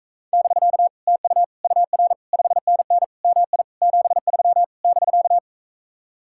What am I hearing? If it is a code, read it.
Morse code, heard as a keyed tone, about 44 wpm, 700 Hz. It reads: <BK> TU UR 5NN MI 73 <BK>